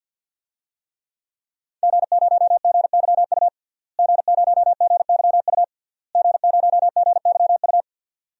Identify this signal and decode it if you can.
Morse code, heard as a keyed tone, about 51 wpm, 700 Hz. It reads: G0GYU G0GYU G0GYU
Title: DXCC-100 Call Signs - 5 Characters - Repeated 3x - 50wpm